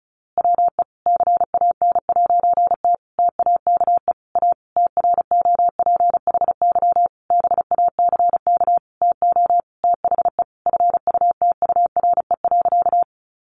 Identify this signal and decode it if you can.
Morse code, heard as a keyed tone, about 34 wpm, 700 Hz. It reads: WE CAN'T TAKE A TROPHY BACK TO THE FUTURE.
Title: A Sound Of Thunder 35wpm